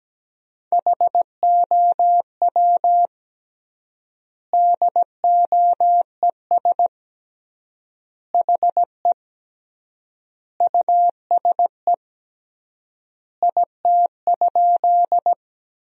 Morse code, heard as a keyed tone, about 17 wpm, 700 Hz. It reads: HOW DOES HE USE IT?
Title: 3x Word-Spacing - Sentences from Top 200 Words - 17wpm